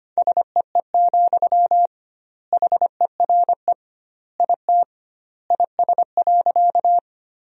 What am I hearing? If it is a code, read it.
Morse code, heard as a keyed tone, about 25 wpm, 700 Hz. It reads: SEE, HERE IT IS.
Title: 2x Word-Spacing - Sentences from Top 200 Words - 25wpm